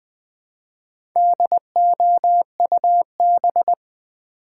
Morse code, heard as a keyed tone, about 20 wpm, 700 Hz. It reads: DOUB
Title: Build A Word - 1 Letter At A Time - 5-Letter Words - 20wpm